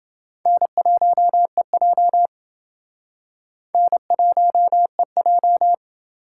Morse code, heard as a keyed tone, about 28 wpm, 700 Hz. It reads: N1EJ N1EJ